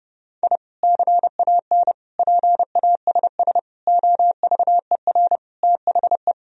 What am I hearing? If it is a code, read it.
Morse code, heard as a keyed tone, about 30 wpm, 700 Hz. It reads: I CAN PASS OVER THE
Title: Sentences from Top 1000 Words - Review Part 1 - 30wpm